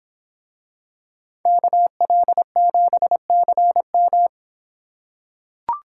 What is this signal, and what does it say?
Morse code, heard as a keyed tone, about 26 wpm, 700 Hz. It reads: KL7CM